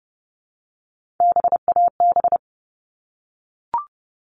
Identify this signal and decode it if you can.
Morse code, heard as a keyed tone, about 30 wpm, 700 Hz. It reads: BAB
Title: All Cut Numbers - 3-Digit 30wpm